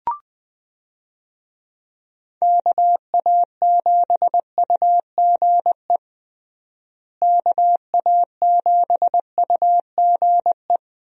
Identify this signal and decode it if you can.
Morse code, heard as a keyed tone, about 20 wpm, 700 Hz. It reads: KA7UGE KA7UGE
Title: USA Call Signs - 6 Characters - Repeated 3x - 20wpm